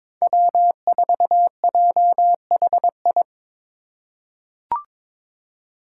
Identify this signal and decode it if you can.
Morse code, heard as a keyed tone, about 22 wpm, 700 Hz. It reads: W4JHI
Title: Five Challenging Letter-Number Combinations 22wpm